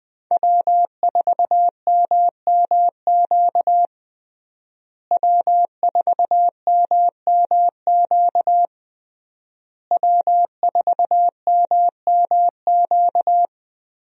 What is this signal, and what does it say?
Morse code, heard as a keyed tone, about 21 wpm, 700 Hz. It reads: W4MMQ W4MMQ W4MMQ
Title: USA Call Signs - 5 Characters - Repeated 3x - 20wpm